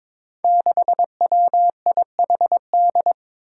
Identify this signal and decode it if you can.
Morse code, heard as a keyed tone, about 22 wpm, 700 Hz. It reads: 6WIHD
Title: Five Challenging Letter-Number Combinations 22wpm